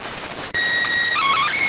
SSTV (slow-scan television), PD 290.